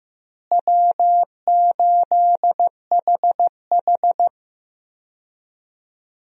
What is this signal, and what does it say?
Morse code, heard as a keyed tone, about 15 wpm, 700 Hz. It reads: W8HH
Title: USA Call Signs - 4 Characters - Repeated 3x - 15wpm